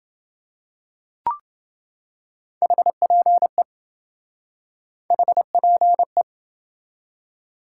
Morse code, heard as a keyed tone, about 28 wpm, 700 Hz. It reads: HPE HPE